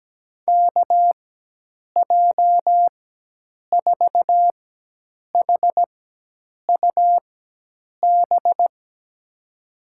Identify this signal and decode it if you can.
Morse code, heard as a keyed tone, about 17 wpm, spaced as for 8 wpm, 700 Hz. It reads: KJ4HUB
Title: USA Call Signs - 6 Characters - 4x Character Spacing - 17wpm